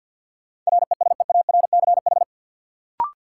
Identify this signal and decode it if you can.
Morse code, heard as a keyed tone, about 46 wpm, 700 Hz. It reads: RESEARCH
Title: Sentences from Top 1400 Words - New Word Drill Alt - 50wpm